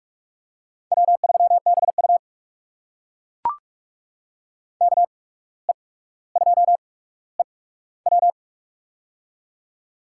Morse code, heard as a keyed tone, about 42 wpm, spaced as for 18 wpm, 700 Hz. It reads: W3BU KE2EW
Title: USA Call Signs - 8x Character Spacing - 45wpm